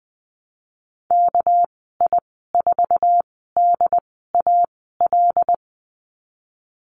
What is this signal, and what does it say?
Morse code, heard as a keyed tone, about 20 wpm, 700 Hz. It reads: K I 4 D A L
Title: 207 Lesson 33g - Letter j - Call Signs 20wpm